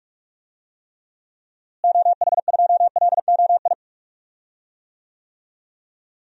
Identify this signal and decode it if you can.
Morse code, heard as a keyed tone, about 46 wpm, 700 Hz. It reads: OH2LOI
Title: DXCC-100 Call Signs - 6 Characters - Repeated 3x - 45wpm